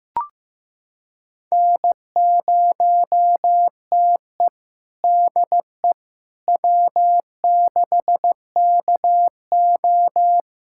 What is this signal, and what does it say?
Morse code, heard as a keyed tone, about 15 wpm, 700 Hz. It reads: N0TE DE W6KO
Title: USA Callsign DE Callsign - 4 Characters - 15wpm